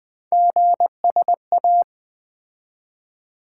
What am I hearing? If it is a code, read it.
Morse code, heard as a keyed tone, about 21 wpm, 700 Hz. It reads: GSA
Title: Three Letter-Number Combinations - 20wpm